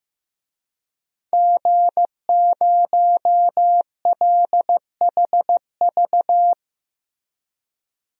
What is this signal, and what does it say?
Morse code, heard as a keyed tone, about 15 wpm, 700 Hz. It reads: G0LHV
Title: DXCC-100 Call Signs - 5 Characters - Repeated 3x - 15wpm